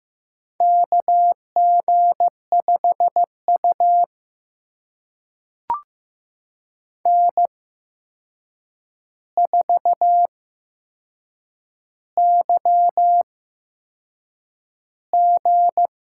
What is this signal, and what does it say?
Morse code, heard as a keyed tone, about 15 wpm, spaced as for 6 wpm, 700 Hz. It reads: KG5U N4YG
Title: USA Call Signs - 4 Characters - 8x Character Spacing - 15wpm